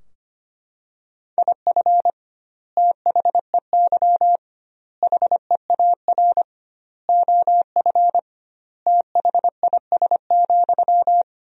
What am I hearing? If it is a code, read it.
Morse code, heard as a keyed tone, about 25 wpm, 700 Hz. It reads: IF THEY HEAR OF THIS,
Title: 2x Word-Spacing - Sentences from Top 400 Words - 25wpm